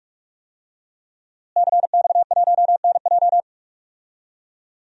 Morse code, heard as a keyed tone, about 47 wpm, 700 Hz. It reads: CX1NJ
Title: DXCC-100 Call Signs - 5 Characters - Repeated 2x - 45wpm